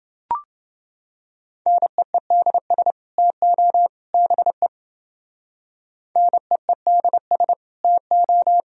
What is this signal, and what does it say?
Morse code, heard as a keyed tone, about 28 wpm, 700 Hz. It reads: NEEDS TO BE NEEDS TO